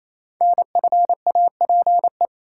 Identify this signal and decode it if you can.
Morse code, heard as a keyed tone, about 27 wpm, 700 Hz. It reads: NFAPE
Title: Code Groups - 5 Letters - 28wpm